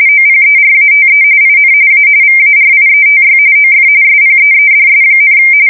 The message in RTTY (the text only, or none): OPERATE IN A VARIETY OF SPEEDS,